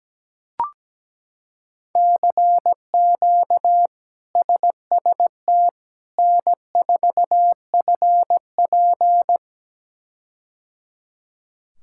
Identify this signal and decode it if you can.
Morse code, heard as a keyed tone, about 17 wpm, 700 Hz. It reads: CQ SST N4FP
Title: K1USN SST QSO Practice - 17wpm